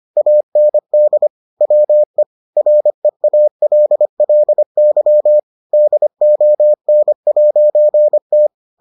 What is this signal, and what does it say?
Morse code, heard as a keyed tone, about 25 wpm, 600 Hz. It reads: AND WE REALLY DON'T